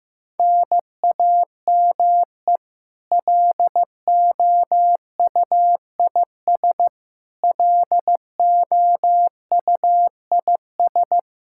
Morse code, heard as a keyed tone, about 15 wpm, 700 Hz. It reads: NAME LOUIS LOUIS
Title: K1USN SST QSO Practice 15wpm